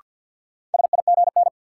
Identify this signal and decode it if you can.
Morse code, heard as a keyed tone, about 48 wpm, 700 Hz. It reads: SIGN
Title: Sentences from Top 1000 Words - New Word Drill - 50wpm